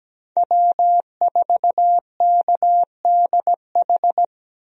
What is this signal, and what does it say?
Morse code, heard as a keyed tone, about 17 wpm, 700 Hz. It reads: W4KDH
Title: USA Call Signs - 5 Characters - Repeated 2x - 17wpm